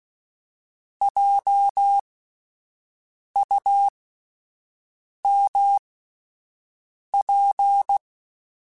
Morse code, heard as a keyed tone, about 16 wpm, spaced as for 5 wpm, 800 Hz. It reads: JUMP